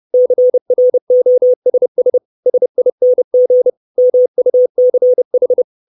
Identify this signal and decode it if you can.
Morse code, heard as a keyed tone, about 30 wpm, 500 Hz. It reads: CROSS SING MUCH
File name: 632 Words 30WPM